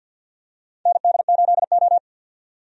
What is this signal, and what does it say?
Morse code, heard as a keyed tone, about 46 wpm, 700 Hz. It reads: ND8O
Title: USA Call Signs - 4 Characters - 50wpm